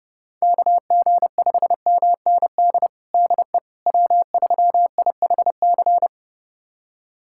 Morse code, heard as a keyed tone, about 30 wpm, 700 Hz. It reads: KG5MND DE W3IHC
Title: USA Callsign DE Callsign 30wpm